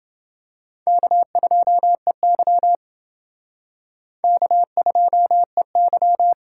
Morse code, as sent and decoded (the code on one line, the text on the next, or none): -.- ..--- . -.-- / -.- ..--- . -.--
K2EY K2EY